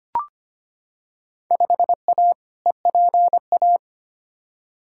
Morse code, heard as a keyed tone, about 25 wpm, 700 Hz. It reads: 5A EPA
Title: Field Day Exchanges - 25wpm